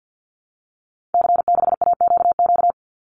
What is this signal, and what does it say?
Morse code, heard as a keyed tone, about 48 wpm, 700 Hz. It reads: C6AQQ
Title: Call Signs Repeated 2x - 50wpm